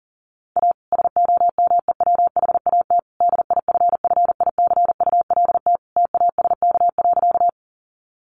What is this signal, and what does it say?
Morse code, heard as a keyed tone, about 40 wpm, 700 Hz. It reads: A SOMEWHAT DIFFICULT TASK.
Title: Sentences from Top 1300 Words 40wpm